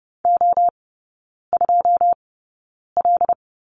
Morse code, heard as a keyed tone, about 30 wpm, spaced as for 9 wpm, 700 Hz. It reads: O2L
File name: 081 Lesson 14e - Number 2 - Three Characters 30wpm